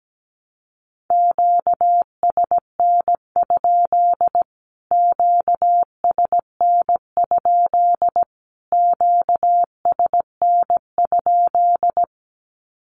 Morse code, heard as a keyed tone, about 17 wpm, 700 Hz. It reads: QSN? QSN? QSN?
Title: Q-Codes Repeated 3x 17wpm